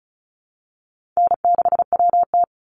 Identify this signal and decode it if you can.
Morse code, heard as a keyed tone, about 34 wpm, 700 Hz. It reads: N6WT
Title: NA SOTA Chaser Callsigns 35wpm